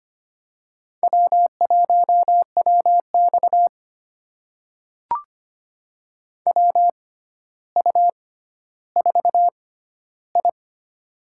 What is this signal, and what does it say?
Morse code, heard as a keyed tone, about 25 wpm, spaced as for 13 wpm, 700 Hz. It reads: W1WX WU4I
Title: USA Call Signs - 4 Characters - 6x Character Spacing - 25wpm